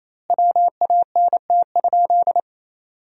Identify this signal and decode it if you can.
Morse code, heard as a keyed tone, about 28 wpm, 700 Hz. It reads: WANT?
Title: 2x Word-Spacing - Sentences from Top 200 Words - 28wpm